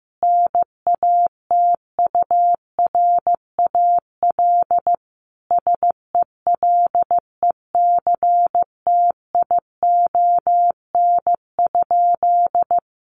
Morse code, heard as a keyed tone, about 15 wpm, 700 Hz. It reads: NATURAL SELECTION?
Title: Sentences from Top 1400 Words 15wpm